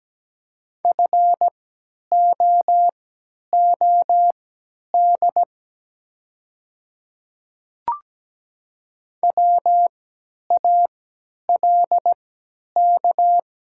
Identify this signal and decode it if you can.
Morse code, heard as a keyed tone, about 17 wpm, spaced as for 10 wpm, 700 Hz. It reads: FOOD WALK
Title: 3x Character Spacing - Top 300 Words - 17wpm